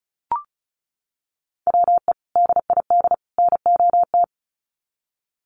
Morse code, heard as a keyed tone, about 34 wpm, 700 Hz. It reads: WE DID NOT
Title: Sets of 3 Words 35wpm